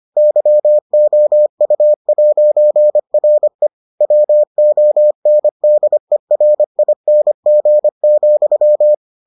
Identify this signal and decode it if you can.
Morse code, heard as a keyed tone, about 25 wpm, 600 Hz. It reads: YOU'RE WONDERING,